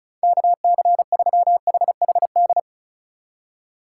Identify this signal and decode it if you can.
Morse code, heard as a keyed tone, about 36 wpm, 700 Hz. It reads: KC3HHD
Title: USA Call Signs - 6 Characters - 8x Character Spacing - 35wpm